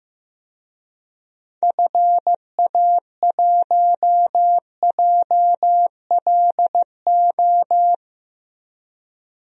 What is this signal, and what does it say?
Morse code, heard as a keyed tone, about 15 wpm, 700 Hz. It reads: FA1JLO
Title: DXCC-100 Call Signs - 6 Characters - Repeated 2x - 15wpm